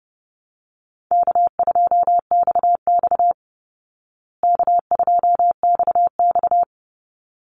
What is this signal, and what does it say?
Morse code, heard as a keyed tone, about 30 wpm, 700 Hz. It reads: K2XX K2XX
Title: Easy Call Signs Repeated 2x - 30wpm